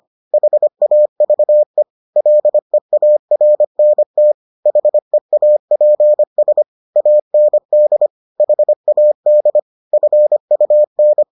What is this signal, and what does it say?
Morse code, heard as a keyed tone, about 25 wpm, 600 Hz. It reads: HAVE LEARNT HEAPS AND HAD FUN